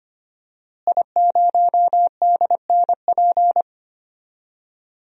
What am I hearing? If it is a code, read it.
Morse code, heard as a keyed tone, about 25 wpm, 700 Hz. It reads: I0DNP
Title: DXCC-100 Call Signs - 5 Characters - Repeated 3x - 25wpm